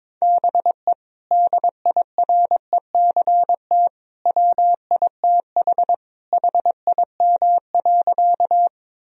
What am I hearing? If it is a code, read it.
Morse code, heard as a keyed tone, about 22 wpm, 700 Hz. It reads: BE DIRECT WITH HIM.
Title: Sentences from Top 1000 Words - Review Part 2 - 22wpm